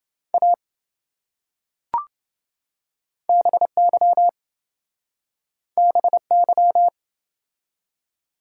Morse code, heard as a keyed tone, about 27 wpm, 700 Hz. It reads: A BY BY